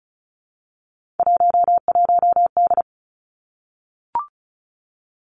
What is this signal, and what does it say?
Morse code, heard as a keyed tone, about 36 wpm, 700 Hz. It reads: 11D
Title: Three Challenging Letter-Number Combinations 35wpm